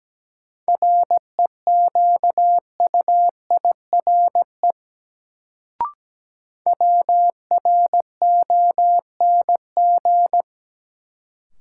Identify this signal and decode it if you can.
Morse code, heard as a keyed tone, about 17 wpm, 700 Hz. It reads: REQUIRE WRONG
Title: Sentences from Top 900 Words - New Word Drill - 17wpm